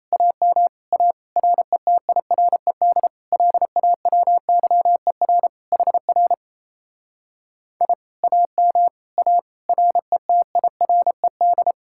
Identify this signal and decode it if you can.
Morse code, heard as a keyed tone, about 38 wpm, 700 Hz. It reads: AM A RETIRED LAWYER HR I AM A RETIRED